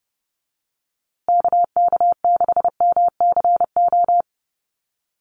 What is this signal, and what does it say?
Morse code, heard as a keyed tone, about 29 wpm, 700 Hz. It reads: KK6MCO
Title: USA Call Signs Repeated 3x 30wpm